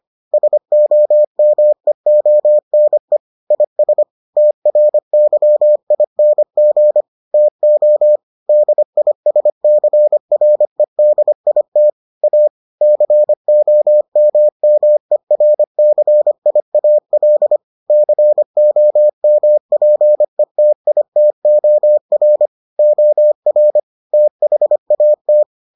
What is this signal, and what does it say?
Morse code, heard as a keyed tone, about 25 wpm, 600 Hz. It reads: SOMEONE IS TRYING TO DISCREDIT A COMMERCIAL COMPETITOR OR THAT